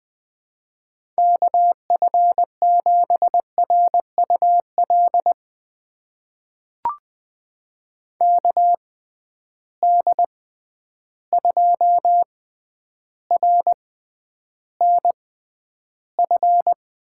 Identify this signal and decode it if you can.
Morse code, heard as a keyed tone, about 20 wpm, spaced as for 10 wpm, 700 Hz. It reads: KF7RUL KD2RNF
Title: USA Call Signs - 6 Characters - 6x Character Spacing - 20wpm